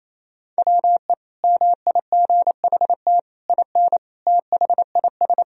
Morse code, heard as a keyed tone, about 28 wpm, 700 Hz. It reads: WE MIGHT IN THIS
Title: Sentences from Top 1000 Words - Review Part 1 - 28wpm